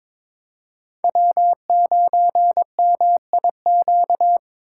Morse code, heard as a keyed tone, about 21 wpm, 700 Hz. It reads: W9MIQ
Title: USA Call Signs - 5 Characters - 6x Character Spacing - 22wpm